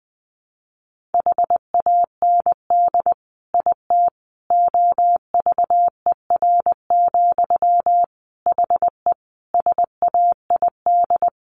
Morse code, heard as a keyed tone, about 19 wpm, 700 Hz. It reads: HAND IT OVER, HE SAID
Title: Sentences from Top 1500 Words 20wpm